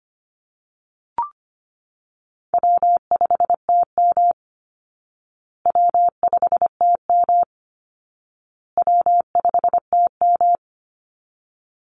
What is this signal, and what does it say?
Morse code, heard as a keyed tone, about 25 wpm, 700 Hz. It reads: W5TM W5TM W5TM
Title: Easy Call Signs Repeated 3x - 25wpm